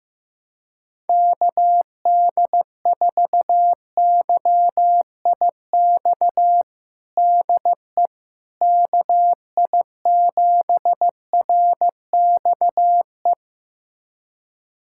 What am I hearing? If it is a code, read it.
Morse code, heard as a keyed tone, about 15 wpm, 700 Hz. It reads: KD4YIX DE KI7RXE
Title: USA Callsign DE Callsign - 6 Characters - 15wpm